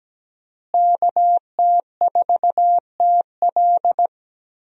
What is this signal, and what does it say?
Morse code, heard as a keyed tone, about 17 wpm, 700 Hz. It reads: KT4TL
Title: USA Call Signs - 5 Characters - Repeated 2x - 17wpm